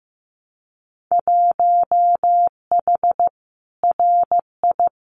Morse code, heard as a keyed tone, about 15 wpm, 700 Hz. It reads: 1H RI
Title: Winter Field Day Exchanges 15wpm